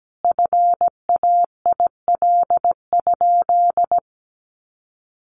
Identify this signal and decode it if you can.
Morse code, heard as a keyed tone, about 17 wpm, 700 Hz. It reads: FAIL?
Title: Sentences from Top 1100 Words 17wpm